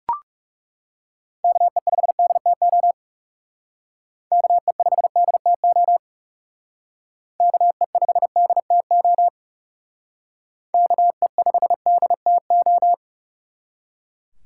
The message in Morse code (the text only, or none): KE5DTO KE5DTO KE5DTO KE5DTO